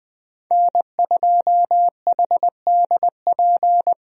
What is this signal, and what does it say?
Morse code, heard as a keyed tone, about 20 wpm, 700 Hz. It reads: N2HDP
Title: USA Call Signs - 5 Characters - Repeated 2x - 20wpm